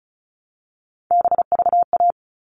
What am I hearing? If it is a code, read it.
Morse code, heard as a keyed tone, about 34 wpm, 700 Hz. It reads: BVA
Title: All Cut Numbers - 3-Digit 35wpm